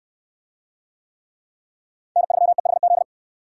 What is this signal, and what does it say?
Morse code, heard as a keyed tone, about 65 wpm, 700 Hz. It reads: N3EHZ